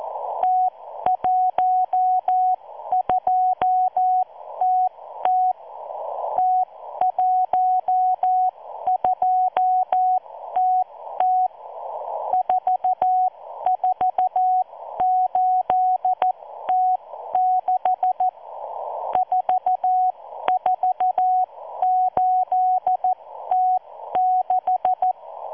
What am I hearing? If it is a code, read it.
Morse code, heard as a keyed tone, about 14 wpm, spaced as for 12 wpm, 740 Hz. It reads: T12TT T12TT 448T6 448T6